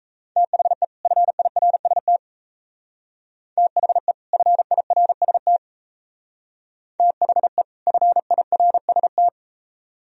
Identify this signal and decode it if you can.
Morse code, heard as a keyed tone, about 36 wpm, 700 Hz. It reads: THE FIRST THE FIRST THE FIRST